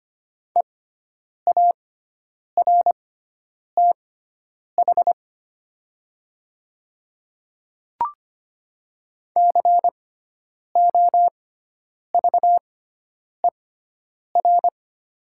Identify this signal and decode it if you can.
Morse code, heard as a keyed tone, about 25 wpm, spaced as for 9 wpm, 690 Hz. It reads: EARTH COVER
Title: 6x Character Spacing - Top 300 Words - 25wpm